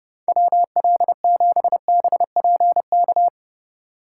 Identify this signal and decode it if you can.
Morse code, heard as a keyed tone, about 30 wpm, 700 Hz. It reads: WL7BPK